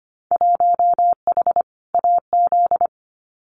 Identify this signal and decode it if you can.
Morse code, heard as a keyed tone, about 25 wpm, 700 Hz. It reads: 1H AZ
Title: Winter Field Day Exchanges 25wpm